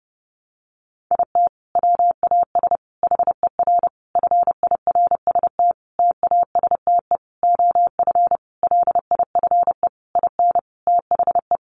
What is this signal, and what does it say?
Morse code, heard as a keyed tone, about 30 wpm, 700 Hz. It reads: IT WAS HER FIRST TASTE OF LIFE IN THE
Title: Sentences from Top 1100 Words 30wpm